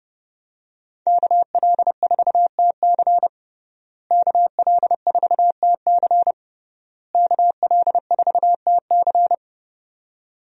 Morse code, heard as a keyed tone, about 30 wpm, 700 Hz. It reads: KL4TC KL4TC KL4TC
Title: USA Call Signs - 5 Characters - Repeated 3x - 30wpm